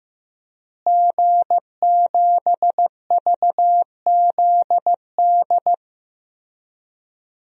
Morse code, heard as a keyed tone, about 15 wpm, 700 Hz. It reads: G7VZD
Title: DXCC-100 Call Signs - 5 Characters - Repeated 3x - 15wpm